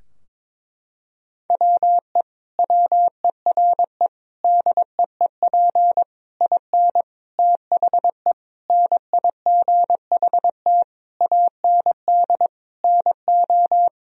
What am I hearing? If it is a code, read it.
Morse code, heard as a keyed tone, about 22 wpm, 700 Hz. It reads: WE WERE DEEP IN THE NIGHT AND NO
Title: Sentences from Top 1000 Words - Review Part 3 - 22wpm